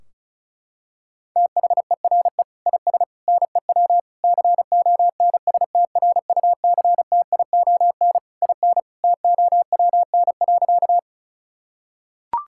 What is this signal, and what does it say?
Morse code, heard as a keyed tone, about 36 wpm, 700 Hz. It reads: THERE IS NEW CONSTRUCTION IN TOWN.
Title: Sentences from Top 1600 Words 35wpm